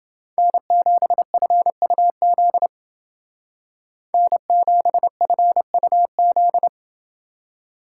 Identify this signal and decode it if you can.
Morse code, heard as a keyed tone, about 29 wpm, 700 Hz. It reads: N7FUZ N7FUZ